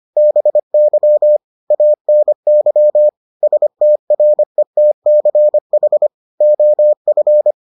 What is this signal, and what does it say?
Morse code, heard as a keyed tone, about 25 wpm, 600 Hz. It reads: BY ANY STRETCH OF